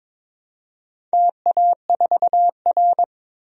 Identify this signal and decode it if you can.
Morse code, heard as a keyed tone, about 22 wpm, 700 Hz. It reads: TA4R
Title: DXCC-100 Call Signs - 4 Characters - 22wpm